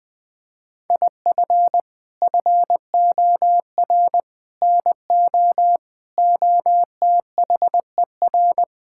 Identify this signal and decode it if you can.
Morse code, heard as a keyed tone, about 20 wpm, 700 Hz. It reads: IF FOR NO OTHER
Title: Sets of 6 Words - 20wpm